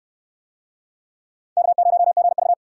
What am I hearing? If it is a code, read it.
Morse code, heard as a keyed tone, about 67 wpm, 700 Hz. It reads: K0GV